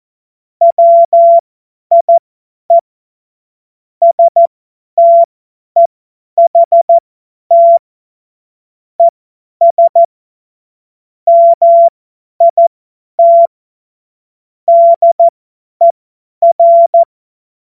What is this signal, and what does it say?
Morse code, heard as a keyed tone, about 14 wpm, spaced as for 10 wpm, 680 Hz. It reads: WIE STEHT ES MIT DER
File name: Morsekurs / Witze / Witz 01. November 2022 50 F70